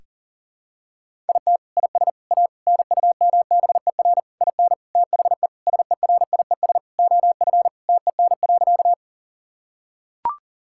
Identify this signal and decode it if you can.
Morse code, heard as a keyed tone, about 37 wpm, 700 Hz. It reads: IT IS A NUMBER IN THE SERIES OF TEN.
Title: Sentences from Top 1000 Words - Review Part 5 - 40wpm